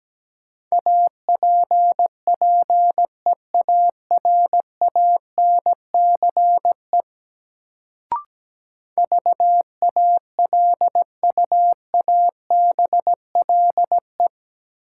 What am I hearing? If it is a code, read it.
Morse code, heard as a keyed tone, about 17 wpm, 700 Hz. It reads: APPEARANCE VALUABLE
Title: Sentences from Top 1100 Words - New Word Drill - 17wpm